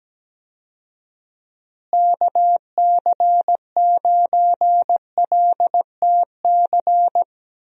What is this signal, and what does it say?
Morse code, heard as a keyed tone, about 17 wpm, 700 Hz. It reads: KC9LTC
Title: USA Call Signs - 6 Characters - Repeated 2x - 17wpm